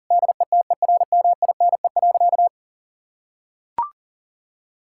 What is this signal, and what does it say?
Morse code, heard as a keyed tone, about 37 wpm, 700 Hz. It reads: DETERMINE.
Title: Sentences from Top 1000 Words - Review Part 5 - 40wpm